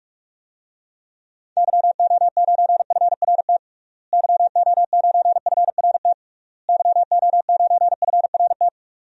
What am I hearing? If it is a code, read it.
Morse code, heard as a keyed tone, about 46 wpm, 700 Hz. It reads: YO9FRT YO9FRT YO9FRT
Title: DXCC-100 Call Signs - 6 Characters - Repeated 3x - 45wpm